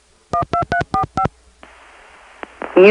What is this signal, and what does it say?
DTMF, heard as telephone touch tones, tones of about 100 ms. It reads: 12345